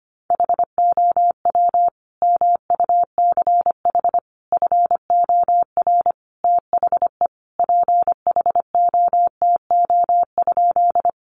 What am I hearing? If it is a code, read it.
Morse code, heard as a keyed tone, about 25 wpm, 700 Hz. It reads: HOW MUCH FOR THE PHOTO?
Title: Sentences from Top 1200 Words 25wpm